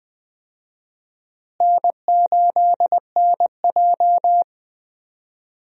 Morse code, heard as a keyed tone, about 20 wpm, 700 Hz. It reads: N8NJ
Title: USA Call Signs - 4 Characters - Repeated 2x - 20wpm